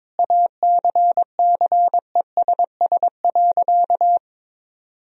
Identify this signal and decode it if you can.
Morse code, heard as a keyed tone, about 22 wpm, 700 Hz. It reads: ACCESS.
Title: Sentences from Top 1000 Words - Review Part 4 - 22wpm